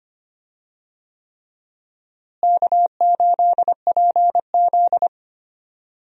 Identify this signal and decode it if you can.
Morse code, heard as a keyed tone, about 25 wpm, 700 Hz. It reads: K8PZ
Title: USA Call Signs - 4 Characters - Repeated 3x - 25wpm